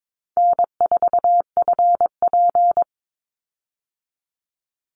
Morse code, heard as a keyed tone, about 22 wpm, 700 Hz. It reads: N4FP
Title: Call Signs Repeated 2x - 22wpm